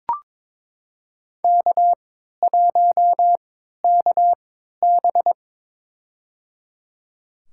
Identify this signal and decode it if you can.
Morse code, heard as a keyed tone, about 22 wpm, spaced as for 12 wpm, 700 Hz. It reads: K1KB
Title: USA Call Signs - 3x Character Spacing - 22wpm